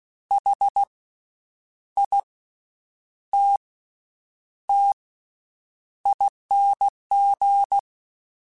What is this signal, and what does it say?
Morse code, heard as a keyed tone, about 16 wpm, spaced as for 8 wpm, 800 Hz. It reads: HITTING
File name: Special INGs-mp3s / Special INGs-mp3s / Slow ing @15